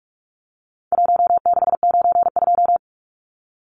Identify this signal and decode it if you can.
Morse code, heard as a keyed tone, about 44 wpm, 700 Hz. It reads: 1692
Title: Numbers - 4-Digit 45wpm